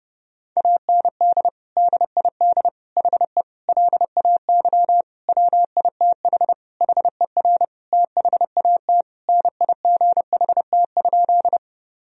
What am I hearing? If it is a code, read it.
Morse code, heard as a keyed tone, about 30 wpm, 700 Hz. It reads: AND DID HE LAY WITH HER THAT NIGHT?
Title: Sentences from Top 1000 Words - Review Part 2 - 30wpm